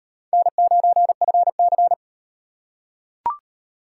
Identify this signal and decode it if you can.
Morse code, heard as a keyed tone, about 33 wpm, 700 Hz. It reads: N9FC